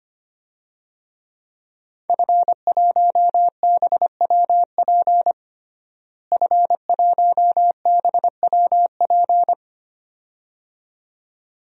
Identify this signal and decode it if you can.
Morse code, heard as a keyed tone, about 25 wpm, 700 Hz. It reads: F1BWP F1BWP
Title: DXCC-100 Call Signs - 5 Characters - Repeated 2x - 25wpm